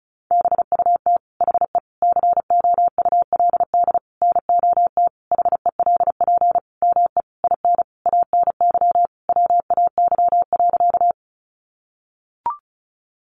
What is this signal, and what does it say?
Morse code, heard as a keyed tone, about 34 wpm, 700 Hz. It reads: BUT HE COULD NOT HELP ME IN ANY WAY.
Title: Sentences from Top 600 Words 35wpm